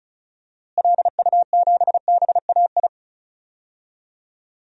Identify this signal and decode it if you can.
Morse code, heard as a keyed tone, about 36 wpm, 700 Hz. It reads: LU7BAI
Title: DXCC-100 Call Signs - 6 Characters - Repeated 2x - 35wpm